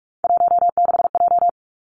Morse code, heard as a keyed tone, about 46 wpm, 700 Hz. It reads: 16J
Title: Three Challenging Letter-Number Combinations 45wpm